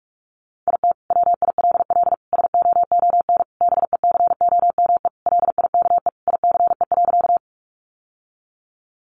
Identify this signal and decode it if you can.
Morse code, heard as a keyed tone, about 44 wpm, 700 Hz. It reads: IT WILL SOON BECOME LIKE ICE.
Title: Sentences from Top 1000 Words 45wpm